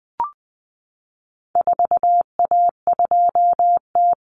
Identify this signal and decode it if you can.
Morse code, heard as a keyed tone, about 20 wpm, 700 Hz. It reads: 4A2T
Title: Common Cut Numbers - 4-Digit 20wpm